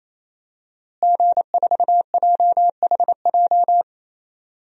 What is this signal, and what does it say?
Morse code, heard as a keyed tone, about 27 wpm, 700 Hz. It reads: G4JHJ
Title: DXCC-100 Call Signs - 5 Characters - Repeated 3x - 28wpm